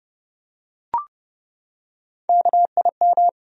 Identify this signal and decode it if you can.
Morse code, heard as a keyed tone, about 32 wpm, 700 Hz. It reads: KIM
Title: USA Common Names - Max 4 Letters - 30wpm